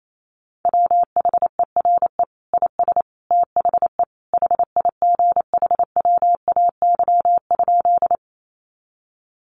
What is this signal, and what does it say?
Morse code, heard as a keyed tone, about 28 wpm, 700 Hz. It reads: WHERE IS THE HIGHWAY?
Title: Sentences from Top 1400 Words 28wpm